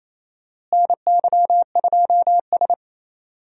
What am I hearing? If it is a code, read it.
Morse code, heard as a keyed tone, about 27 wpm, 700 Hz. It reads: NY2S
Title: USA Call Signs - 4 Characters - Repeated 2x - 28wpm